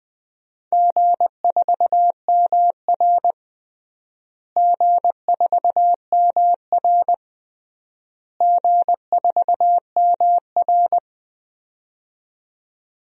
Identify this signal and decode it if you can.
Morse code, heard as a keyed tone, about 21 wpm, 700 Hz. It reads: G4MR G4MR G4MR
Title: DXCC-100 Call Signs - 4 Characters - Repeated 3x - 20wpm